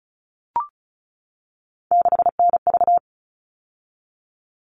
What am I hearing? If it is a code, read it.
Morse code, heard as a keyed tone, about 34 wpm, 700 Hz. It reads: 6NV
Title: All Cut Numbers - 3-Digit 35wpm